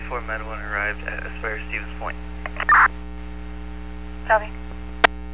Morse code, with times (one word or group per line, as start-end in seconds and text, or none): none